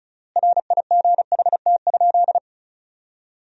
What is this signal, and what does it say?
Morse code, heard as a keyed tone, about 36 wpm, 700 Hz. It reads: RIGHT?
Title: 3x Word-Spacing - Sentences from Top 200 Words - 35wpm